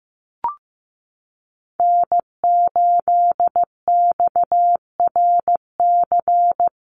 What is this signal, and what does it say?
Morse code, heard as a keyed tone, about 15 wpm, 700 Hz. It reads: N8XRC
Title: USA Call Signs Repeated 3x 15wpm